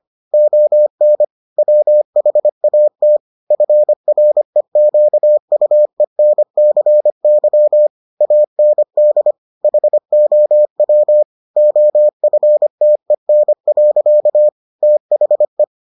Morse code, heard as a keyed tone, about 25 wpm, 600 Hz. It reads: ON WHAT FREQUENCY AND HOW OFTEN. THE